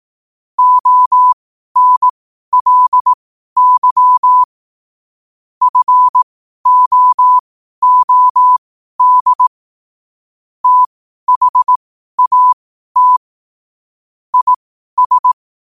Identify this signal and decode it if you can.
Morse code, heard as a keyed tone, about 18 wpm, spaced as for 12 wpm, 1000 Hz. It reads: ONLY FOOD THAT IS
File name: qotdfiles / 12wpm